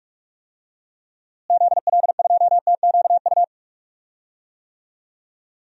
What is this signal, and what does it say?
Morse code, heard as a keyed tone, about 41 wpm, 700 Hz. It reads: ZL2TQU
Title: DXCC-100 Call Signs - 6 Characters - Repeated 2x - 45wpm